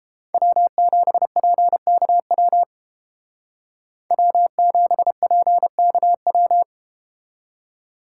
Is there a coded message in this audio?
Yes: Morse code from 0.3 to 6.6 s, heard as a keyed tone, about 33 wpm, 700 Hz. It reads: W7PKW W7PKW